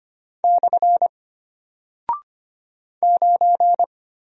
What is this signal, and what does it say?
Morse code, heard as a keyed tone, about 25 wpm, 700 Hz. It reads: / 9
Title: Single Character (Letter:Number:Punctuation:Prosign) Alt 25wpm